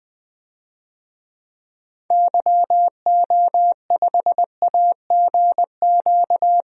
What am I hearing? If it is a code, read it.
Morse code, heard as a keyed tone, about 20 wpm, 700 Hz. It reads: YO5AGQ
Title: DXCC-100 Call Signs - 6 Characters - Repeated 2x - 20wpm